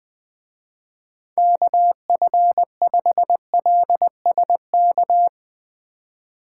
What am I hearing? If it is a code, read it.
Morse code, heard as a keyed tone, about 20 wpm, 700 Hz. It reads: KF5LSK